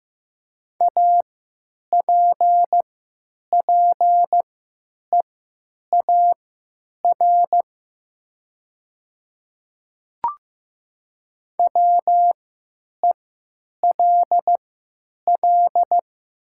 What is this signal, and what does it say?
Morse code, heard as a keyed tone, about 15 wpm, spaced as for 9 wpm, 700 Hz. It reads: APPEAR WELL
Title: 3x Character Spacing - Top 500 Words - 15wpm